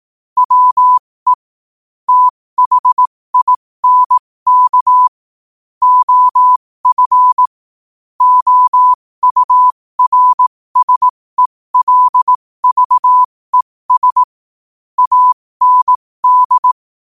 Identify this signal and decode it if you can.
Morse code, heard as a keyed tone, about 18 wpm, spaced as for 15 wpm, 1000 Hz. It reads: WE THINK OF OURSELVES AND